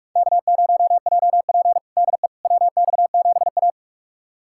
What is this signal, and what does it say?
Morse code, heard as a keyed tone, about 46 wpm, 700 Hz. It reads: K0JP DE WX7A
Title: USA Callsign DE Callsign - 4 Characters - 45wpm